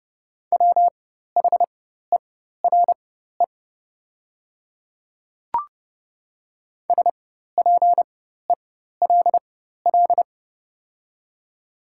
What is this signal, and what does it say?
Morse code, heard as a keyed tone, about 30 wpm, spaced as for 14 wpm, 700 Hz. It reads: WHERE SPELL
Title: 4x Character Spacing - Top 200 Words - 30wpm